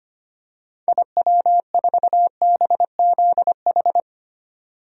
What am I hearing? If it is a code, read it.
Morse code, heard as a keyed tone, about 25 wpm, 700 Hz. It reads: IW4BZH
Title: DXCC-100 Call Signs - 6 Characters - 25wpm